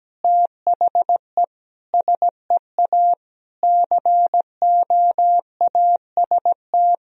Morse code, heard as a keyed tone, about 17 wpm, 700 Hz. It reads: THE SEA COAST
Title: Sentences from Top 1000 Words - Review Part 4 - 17wpm